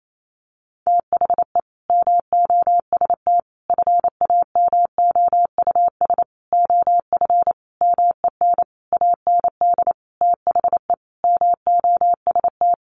Morse code, heard as a keyed tone, about 28 wpm, 700 Hz. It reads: THE MOST FAMOUS OF MEN AND THE MOST
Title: Sentences from Top 1500 Words 28wpm